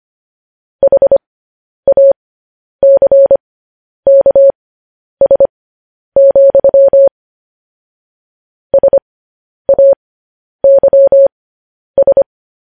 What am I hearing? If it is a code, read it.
Morse code, heard as a keyed tone, about 25 wpm, spaced as for 10 wpm, 550 Hz. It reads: HACKS, SAYS
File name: podcast / News headlines-1x-25x10